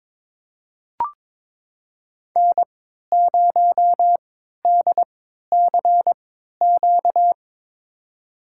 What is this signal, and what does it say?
Morse code, heard as a keyed tone, about 22 wpm, spaced as for 13 wpm, 700 Hz. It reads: N0DCQ
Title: USA Call Signs - 3x Character Spacing - 22wpm